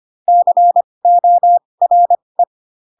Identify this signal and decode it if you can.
Morse code, heard as a keyed tone, about 25 wpm, spaced as for 20 wpm, 700 Hz. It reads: CORE
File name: Pinocchio / Pinocchio ch7 15 25 ik1hge